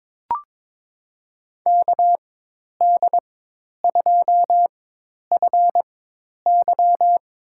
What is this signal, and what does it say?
Morse code, heard as a keyed tone, about 22 wpm, spaced as for 10 wpm, 700 Hz. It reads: KD2FY
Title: USA Call Signs - 4x Character Spacing - 22wpm